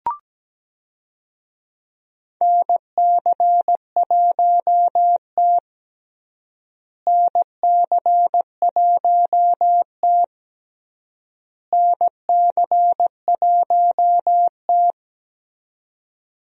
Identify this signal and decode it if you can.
Morse code, heard as a keyed tone, about 17 wpm, 700 Hz. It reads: NC1T NC1T NC1T
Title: USA Call Signs - 4 Characters - Repeated 3x - 17wpm